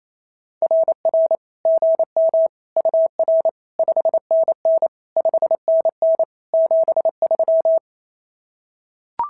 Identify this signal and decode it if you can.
Morse code, heard as a keyed tone, about 28 wpm, 650 Hz. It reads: RR GM UR 5NN 5NN 73
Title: SOTA QSO Practice 28wpm